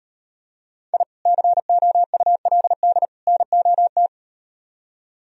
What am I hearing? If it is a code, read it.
Morse code, heard as a keyed tone, about 38 wpm, 700 Hz. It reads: I COULD NOT